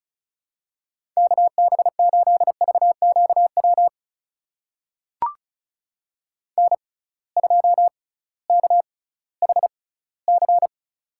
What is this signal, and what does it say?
Morse code, heard as a keyed tone, about 34 wpm, spaced as for 19 wpm, 700 Hz. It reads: KB8VQW N2KHC
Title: USA Call Signs - 6x Character Spacing - 35wpm